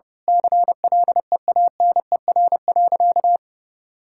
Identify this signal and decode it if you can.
Morse code, heard as a keyed tone, about 32 wpm, 700 Hz. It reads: CLEANER.
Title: Pangram Challenge v2 - 30wpm